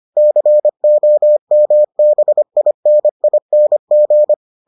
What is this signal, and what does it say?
Morse code, heard as a keyed tone, about 25 wpm, 600 Hz. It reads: COMBINING